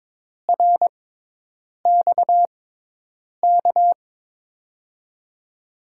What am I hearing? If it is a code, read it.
Morse code, heard as a keyed tone, about 22 wpm, spaced as for 8 wpm, 700 Hz. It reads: RXK
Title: USA Call Signs - 6x Character Spacing - 22wpm